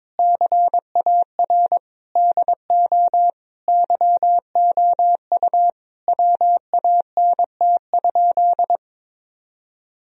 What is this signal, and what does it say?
Morse code, heard as a keyed tone, about 22 wpm, 700 Hz. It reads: CAR DO YOU WANT?
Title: Sentences from Top 1000 Words - Review Part 1 - 22wpm